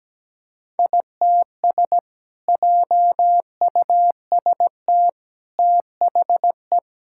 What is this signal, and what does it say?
Morse code, heard as a keyed tone, about 17 wpm, 700 Hz. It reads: ITS JUST THE
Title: Sentences from Top 1000 Words - Review Part 5 - 17wpm